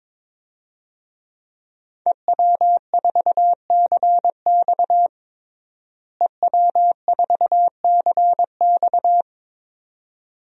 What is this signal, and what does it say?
Morse code, heard as a keyed tone, about 22 wpm, 700 Hz. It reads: EW4CX EW4CX
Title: DXCC-100 Call Signs - 5 Characters - Repeated 2x - 22wpm